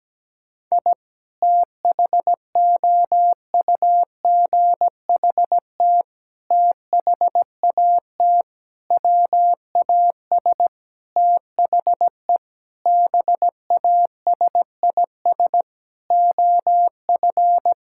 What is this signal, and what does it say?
Morse code, heard as a keyed tone, about 17 wpm, 700 Hz. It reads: I THOUGHT THAT WAS THE BASIS OF
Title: Sentences from Top 1000 Words - Review Part 5 - 17wpm